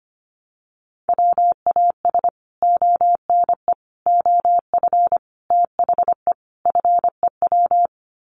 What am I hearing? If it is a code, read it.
Morse code, heard as a keyed tone, about 25 wpm, 700 Hz. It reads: WAS ONE OF THE FEW
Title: Sets of 5 Words Encore 25wpm